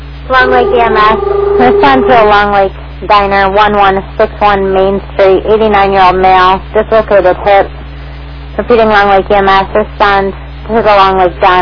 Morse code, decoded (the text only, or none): none